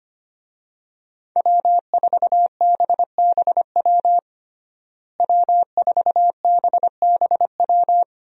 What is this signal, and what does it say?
Morse code, heard as a keyed tone, about 25 wpm, 700 Hz. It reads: W4BBW W4BBW
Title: USA Call Signs - 5 Characters - Repeated 3x - 25wpm